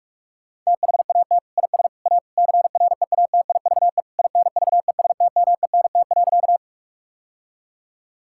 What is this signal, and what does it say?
Morse code, heard as a keyed tone, about 42 wpm, 700 Hz. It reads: THAT IS A CREATIVE INVESTMENT.
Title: Sentences from Top 1000 Words - Review Part 5 - 45wpm